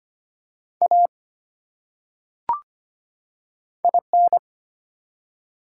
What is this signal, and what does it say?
Morse code, heard as a keyed tone, about 25 wpm, 700 Hz. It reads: A IN
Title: Top 180 QSO Elements 25wpm